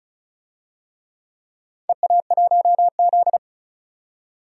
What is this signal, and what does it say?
Morse code, heard as a keyed tone, about 36 wpm, 700 Hz. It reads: EA1Z
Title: DXCC-100 Call Signs - 4 Characters - 35wpm